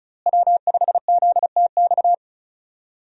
Morse code, heard as a keyed tone, about 34 wpm, 700 Hz. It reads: W5ZTX
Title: USA Call Signs - 5 Characters - Repeated 2x - 35wpm